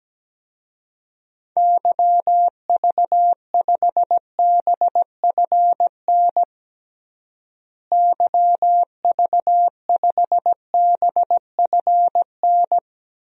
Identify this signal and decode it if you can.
Morse code, heard as a keyed tone, about 17 wpm, 700 Hz. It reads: YV5BFN YV5BFN
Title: DXCC-100 Call Signs - 6 Characters - Repeated 2x - 17wpm